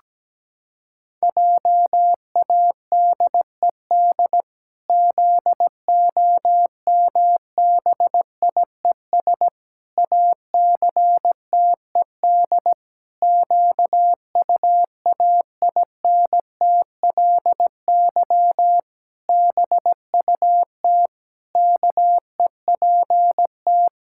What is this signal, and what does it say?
Morse code, heard as a keyed tone, about 17 wpm, 700 Hz. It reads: JADED ZOMBIES ACTED QUAINTLY BUT KEPT
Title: Pangram Challenge v2 - 17wpm